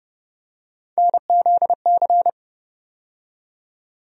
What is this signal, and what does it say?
Morse code, heard as a keyed tone, about 31 wpm, 700 Hz. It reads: NZC
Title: USA Call Sign Postfixes - 3 Letters 30wpm